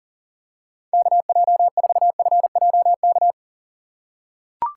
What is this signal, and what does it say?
Morse code, heard as a keyed tone, about 40 wpm, 700 Hz. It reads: KJ4FJK
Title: USA Call Signs - 4x Character Spacing - 40wpm